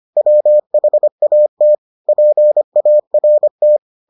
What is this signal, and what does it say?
Morse code, heard as a keyed tone, about 25 wpm, 600 Hz. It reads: WHAT PART